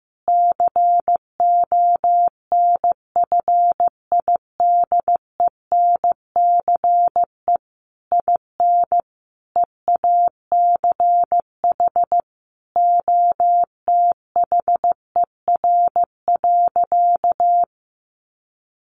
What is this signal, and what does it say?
Morse code, heard as a keyed tone, about 15 wpm, 700 Hz. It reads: CONFIDENCE IN EACH OTHER.
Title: Sentences from Top 1500 Words 15wpm